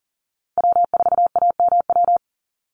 Morse code, heard as a keyed tone, about 40 wpm, 700 Hz. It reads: W4AMW
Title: NA SOTA Chaser Callsigns 40wpm